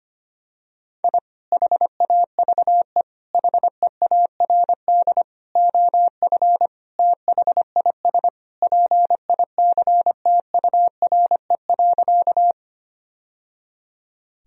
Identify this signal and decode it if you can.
Morse code, heard as a keyed tone, about 25 wpm, 700 Hz. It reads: I HAVE HEARD OF THIS PICTURE.
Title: Sentences from Top 1000 Words - Review Part 1 - 25wpm